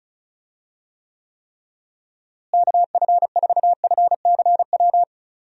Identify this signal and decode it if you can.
Morse code, heard as a keyed tone, about 34 wpm, 700 Hz. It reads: KF4FCW
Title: USA Call Signs - 6 Characters - Repeated 3x - 35wpm